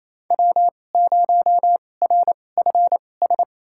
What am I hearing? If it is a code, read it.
Morse code, heard as a keyed tone, about 28 wpm, spaced as for 20 wpm, 700 Hz. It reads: W0RFS
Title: USA Call Signs - 5 Characters - 2x Character Spacing - 28wpm